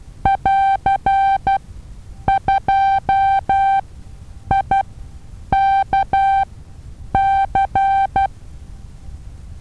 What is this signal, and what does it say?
Morse code, heard as a keyed tone, about 12 wpm, spaced as for 8 wpm, 780 Hz. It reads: <AR>2IKC
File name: Taviro HA6VA B